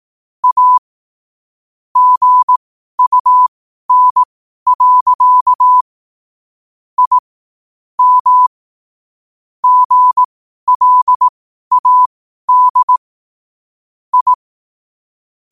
Morse code, heard as a keyed tone, about 18 wpm, spaced as for 12 wpm, 1000 Hz. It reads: A GUN. I M GLAD I
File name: qotdfiles / 12wpm